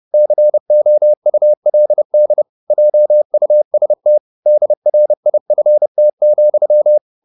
Morse code, heard as a keyed tone, about 30 wpm, 600 Hz. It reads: COULD JUST DRIFT,
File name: the old man and the sea / the old man and the sea 1 30wpm